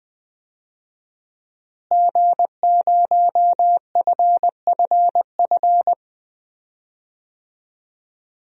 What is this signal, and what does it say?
Morse code, heard as a keyed tone, about 20 wpm, 700 Hz. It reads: G0FFF
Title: DXCC-100 Call Signs - 5 Characters - Repeated 3x - 20wpm